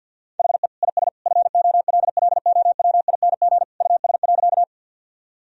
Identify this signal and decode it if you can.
Morse code, heard as a keyed tone, about 50 wpm, 700 Hz. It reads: HE IS FOLLOWING US.
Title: Sentences from Top 1000 Words - Review Part 4 - 50wpm